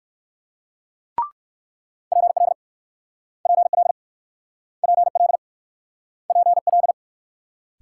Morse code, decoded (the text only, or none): PL PL PL PL